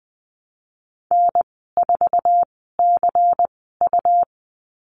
Morse code, heard as a keyed tone, about 20 wpm, spaced as for 14 wpm, 700 Hz. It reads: N4CU
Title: 186 Lesson 30g - Letter b - Call Signs 20wpm